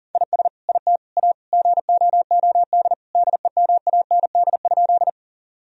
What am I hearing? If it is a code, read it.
Morse code, heard as a keyed tone, about 40 wpm, 700 Hz. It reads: IS IT A GOOD DEMAND?
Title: Sentences from Top 1000 Words - Review Part 4 - 40wpm